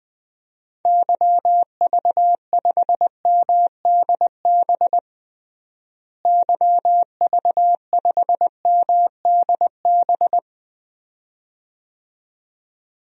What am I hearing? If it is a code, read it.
Morse code, heard as a keyed tone, about 20 wpm, 700 Hz. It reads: YV5MDB YV5MDB
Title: DXCC-100 Call Signs - 6 Characters - Repeated 2x - 20wpm